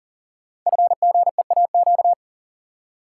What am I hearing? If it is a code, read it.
Morse code, heard as a keyed tone, about 39 wpm, 700 Hz. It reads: FGEAQ
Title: Code Groups - 5 Letters - 40wpm